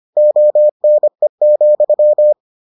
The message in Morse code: ONE,

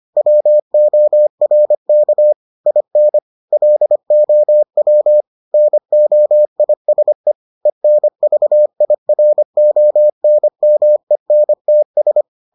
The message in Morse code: WORK IN LOW NOISE ENVIRONMENTS

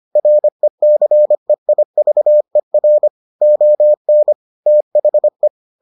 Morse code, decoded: RECEIVER ON THE

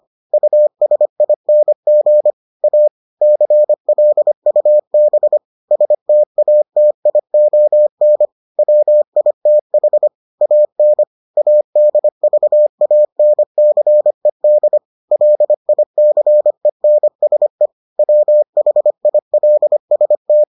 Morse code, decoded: USING A CLUB STATION WITH AN ADVANCED LICENSE WHILST

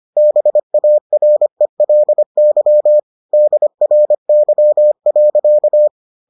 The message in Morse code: BARELY DRY.

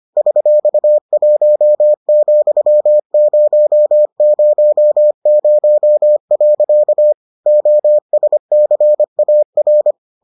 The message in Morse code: $1,000. OSCAR

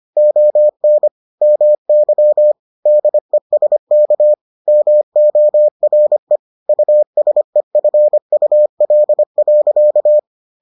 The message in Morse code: ON MY DESK MORE USEFUL.

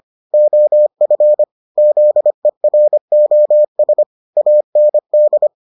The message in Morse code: OF ZEROS AND

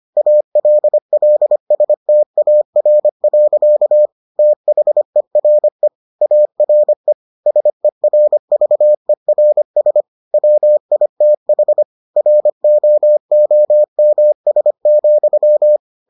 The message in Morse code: ALLSTAR. THERE ARE SERVERS WITH ROOMS,

